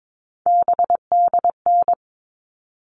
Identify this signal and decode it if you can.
Morse code, heard as a keyed tone, about 22 wpm, 700 Hz. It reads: BDN
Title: All Cut Numbers - 3-Digit 22wpm